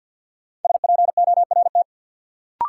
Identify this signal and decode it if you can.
Morse code, heard as a keyed tone, about 48 wpm, 700 Hz. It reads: SPORT
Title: Sentences from Top 1500 Words - New Word Drill Alt - 50wpm